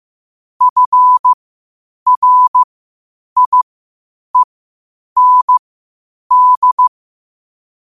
Morse code, heard as a keyed tone, about 15 wpm, spaced as for 8 wpm, 1000 Hz. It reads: FRIEND
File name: qotdfiles / 8wpm